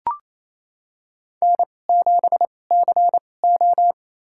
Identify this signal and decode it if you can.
Morse code, heard as a keyed tone, about 28 wpm, spaced as for 20 wpm, 700 Hz. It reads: N7CO
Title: USA Call Signs - 4 Characters - 2x Character Spacing - 28wpm